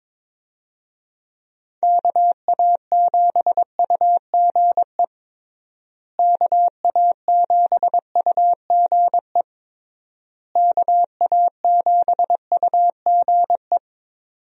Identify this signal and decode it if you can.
Morse code, heard as a keyed tone, about 21 wpm, 700 Hz. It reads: KA7UGE KA7UGE KA7UGE
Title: USA Call Signs - 6 Characters - Repeated 3x - 22wpm